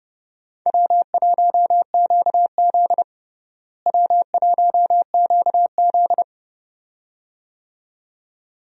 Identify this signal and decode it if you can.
Morse code, heard as a keyed tone, about 32 wpm, 700 Hz. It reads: W1QZ W1QZ
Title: USA Call Signs - 4 Characters - Repeated 2x - 30wpm